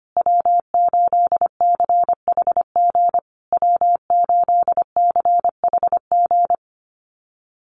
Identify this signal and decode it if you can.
Morse code, heard as a keyed tone, about 25 wpm, 700 Hz. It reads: W8CHG W8CHG
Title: USA Call Signs Repeated 3x 25wpm